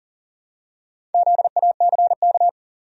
Morse code, heard as a keyed tone, about 40 wpm, 700 Hz. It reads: ZACK
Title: USA Common Names - Max 5 Letters - 40wpm